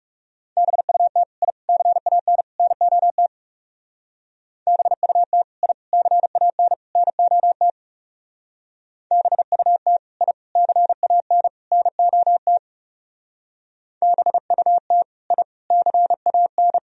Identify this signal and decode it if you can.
Morse code, heard as a keyed tone, about 37 wpm, 700 Hz. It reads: BUT I CAN NOT BUT I CAN NOT BUT I CAN NOT BUT I CAN
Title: Speed-Racing - Sets of 4 Words 30wpm